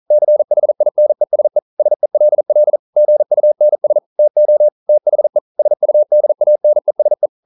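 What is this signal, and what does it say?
Morse code, heard as a keyed tone, about 40 wpm, 600 Hz. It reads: CHINESE SELL GUNS TO THE SUDANESE
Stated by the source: Morse code practice at 41 wpm, from CW med Gnister.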